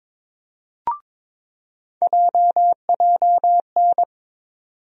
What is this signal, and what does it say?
Morse code, heard as a keyed tone, about 22 wpm, 700 Hz. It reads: JJN
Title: USA Call Sign Postfixes - 3 Letters 22wpm